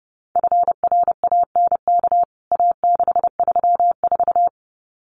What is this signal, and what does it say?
Morse code, heard as a keyed tone, about 30 wpm, 700 Hz. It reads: FRANK A634
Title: CWT Exchange 30wpm